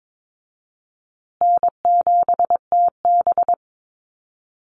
Morse code, heard as a keyed tone, about 22 wpm, 700 Hz. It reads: N7TB
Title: Easy Call Signs Repeated 3x - 22wpm